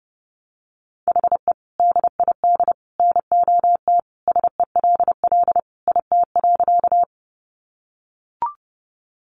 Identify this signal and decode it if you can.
Morse code, heard as a keyed tone, about 30 wpm, 700 Hz. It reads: HE DID NOT SELL IT.
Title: Sentences from Top 1100 Words 30wpm